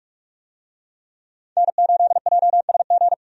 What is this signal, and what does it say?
Morse code, heard as a keyed tone, about 48 wpm, 700 Hz. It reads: N8JSG
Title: USA Call Signs - 5 Characters - 45wpm